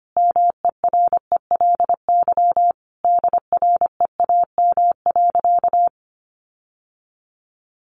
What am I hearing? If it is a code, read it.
Morse code, heard as a keyed tone, about 25 wpm, 700 Hz. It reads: MERELY DREAM.
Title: Sentences from Top 1500 Words 25wpm